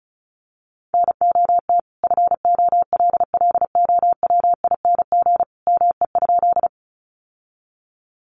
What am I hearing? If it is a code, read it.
Morse code, heard as a keyed tone, about 34 wpm, 700 Hz. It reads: NOT FOLLOWING ME?
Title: Sentences from Top 700 Words 35wpm